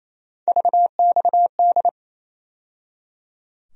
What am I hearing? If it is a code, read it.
Morse code, heard as a keyed tone, about 28 wpm, 700 Hz. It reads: VXD
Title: USA Call Sign Postfixes - 3 Letters 28wpm